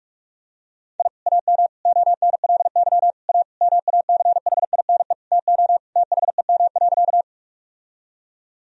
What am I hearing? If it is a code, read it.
Morse code, heard as a keyed tone, about 45 wpm, 700 Hz. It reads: I AM ONLY A MACHINE TO THEM.
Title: Sentences from Top 1000 Words - Review Part 2 - 45wpm